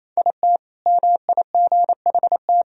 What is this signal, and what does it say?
Morse code, heard as a keyed tone, about 28 wpm, 700 Hz. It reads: IT MIGHT
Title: Sentences from Top 1000 Words - Review Part 3 - 28wpm